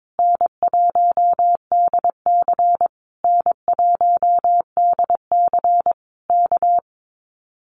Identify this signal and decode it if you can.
Morse code, heard as a keyed tone, about 22 wpm, 700 Hz. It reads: N1DC N1DC K
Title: Common QSO Phrases Encore 22wpm